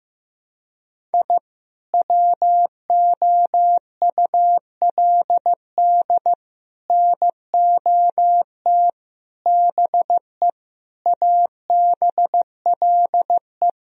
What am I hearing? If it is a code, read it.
Morse code, heard as a keyed tone, about 15 wpm, 700 Hz. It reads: I WOULD NOT BE ABLE
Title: Sentences from Top 1000 Words - Review Part 2 - 15wpm